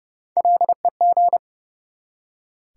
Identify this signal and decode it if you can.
Morse code, heard as a keyed tone, about 30 wpm, 700 Hz. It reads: LEG
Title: Three Letter Words - 30wpm